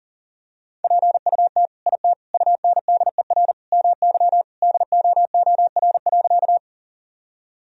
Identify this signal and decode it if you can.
Morse code, heard as a keyed tone, about 40 wpm, 700 Hz. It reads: PUT IT UNDER MY DOOR.
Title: Sentences from Top 1000 Words - Review Part 1 - 40wpm